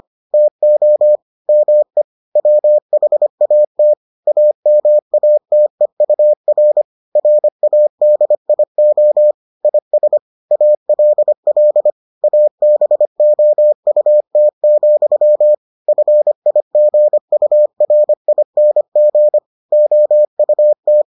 Morse code, heard as a keyed tone, about 25 wpm, 600 Hz. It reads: TO ME WHAT AMATEUR RADIO IS ALL ABOUT, FIGURING OUT